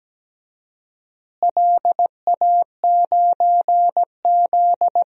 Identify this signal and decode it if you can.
Morse code, heard as a keyed tone, about 17 wpm, 700 Hz. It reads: LA9Z
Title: DXCC-100 Call Signs - 4 Characters - Repeated 2x - 17wpm